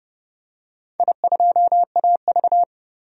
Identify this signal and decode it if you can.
Morse code, heard as a keyed tone, about 30 wpm, 700 Hz. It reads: I2AV
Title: DXCC-100 Call Signs - 4 Characters - 30wpm